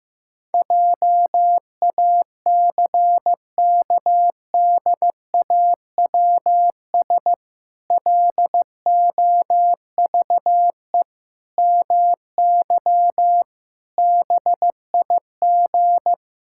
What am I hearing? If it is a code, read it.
Morse code, heard as a keyed tone, about 15 wpm, 700 Hz. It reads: JACKDAWS LOVE MY BIG
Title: Pangram Challenge v2 - 15wpm